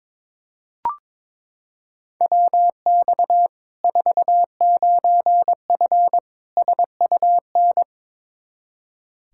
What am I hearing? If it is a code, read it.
Morse code, heard as a keyed tone, about 22 wpm, 700 Hz. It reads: WX 49F SUN
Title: Practice QSO - Exchange 2 - Info - 22wpm